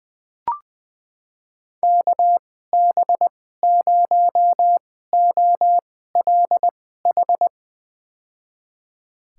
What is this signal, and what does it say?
Morse code, heard as a keyed tone, about 20 wpm, spaced as for 14 wpm, 700 Hz. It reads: KB0OLH
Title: USA Call Signs - 6 Characters - 2x Character Spacing - 20wpm